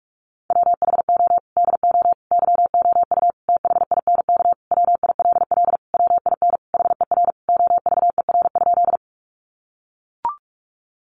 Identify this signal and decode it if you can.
Morse code, heard as a keyed tone, about 44 wpm, 700 Hz. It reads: WHO DO YOU THINK WILL WIN HER OVER?
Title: Sentences from Top 1500 Words 45wpm